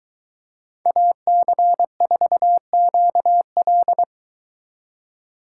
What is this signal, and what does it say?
Morse code, heard as a keyed tone, about 22 wpm, 700 Hz. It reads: AC4QL